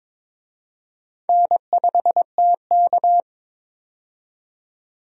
Morse code, heard as a keyed tone, about 22 wpm, 700 Hz. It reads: N5TK
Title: USA Call Signs - 4 Characters - 22wpm